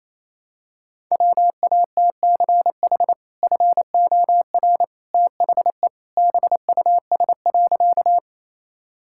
Morse code, heard as a keyed tone, about 28 wpm, 700 Hz. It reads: WATCH FOR THE BUS.
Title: Sentences from Top 1000 Words - Review Part 6 - 28wpm